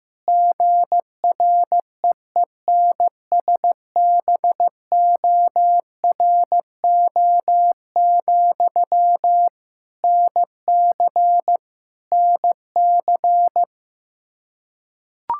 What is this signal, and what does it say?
Morse code, heard as a keyed tone, about 15 wpm, 700 Hz. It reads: GREENSBORO, NC NC
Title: K1USN SST QSO Practice 15wpm